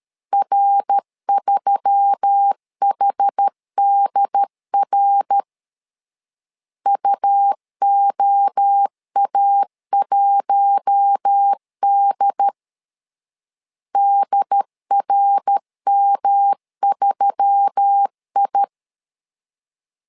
Morse code, heard as a keyed tone, about 13 wpm, 800 Hz. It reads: R3HDR UOA1D DRM3I
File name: LEZIONI / 4 Q1S3 12wpm 2023